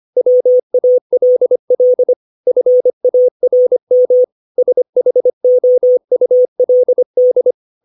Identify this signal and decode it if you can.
Morse code, heard as a keyed tone, about 25 wpm, 500 Hz. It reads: WALL FARM SHOULD
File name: 632 words / 632 Words 25WPM